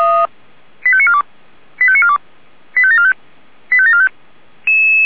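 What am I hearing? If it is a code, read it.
Five-tone selective call (ZVEI), heard as quick runs of five tones, tones of about 70 ms. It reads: 85942 85942 85745 85745
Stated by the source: ZVEI five-tone selective call 85730, received on SDR.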